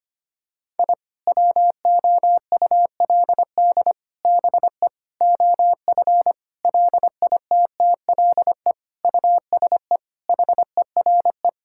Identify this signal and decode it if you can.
Morse code, heard as a keyed tone, about 25 wpm, 700 Hz. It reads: I WOULD BE OF LITTLE USE HERE
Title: Sentences from Top 1000 Words - Review Part 5 - 25wpm